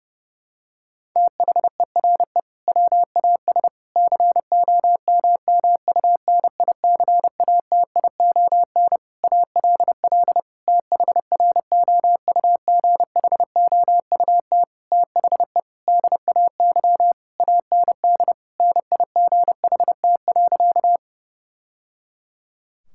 Morse code, heard as a keyed tone, about 30 wpm, 700 Hz. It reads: THERE WAS COMMUNICATION ALL THROUGHOUT THE DAY AND NIGHT.
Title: Sentences from Top 1000 Words - Review Part 6 - 30wpm